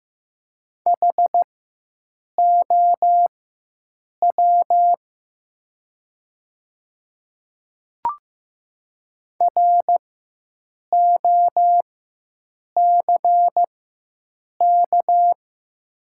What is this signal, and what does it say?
Morse code, heard as a keyed tone, about 15 wpm, spaced as for 7 wpm, 690 Hz. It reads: HOW ROCK
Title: 4x Character Spacing - Top 500 Words - 15wpm